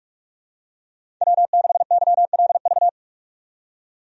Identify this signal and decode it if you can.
Morse code, heard as a keyed tone, about 44 wpm, 700 Hz. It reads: W6YLV
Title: USA Call Signs - 5 Characters - 45wpm